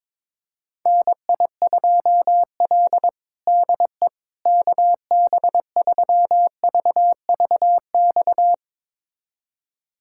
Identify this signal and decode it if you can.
Morse code, heard as a keyed tone, about 22 wpm, 700 Hz. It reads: NI2L DE KB3VVX
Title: USA Callsign DE Callsign 22wpm